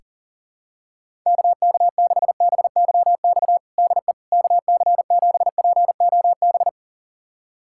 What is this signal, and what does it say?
Morse code, heard as a keyed tone, about 40 wpm, 700 Hz. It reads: KK6BYX DE KC7POB
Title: USA Callsign DE Callsign - 6 Characters - 40wpm